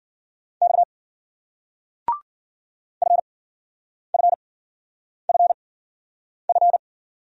Morse code, heard as a keyed tone, about 56 wpm, spaced as for 9 wpm, 700 Hz. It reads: X FFFF